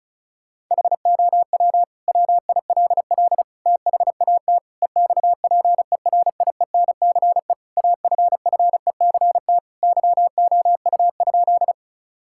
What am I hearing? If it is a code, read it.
Morse code, heard as a keyed tone, about 34 wpm, 690 Hz. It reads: HOW WILL THAT EXPERIENCE AFFECT YOU?
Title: Sentences from Top 1000 Words - Review Part 6 - 35wpm